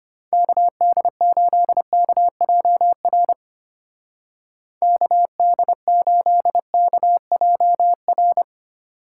Morse code, heard as a keyed tone, about 32 wpm, 700 Hz. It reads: KD8KJR KD8KJR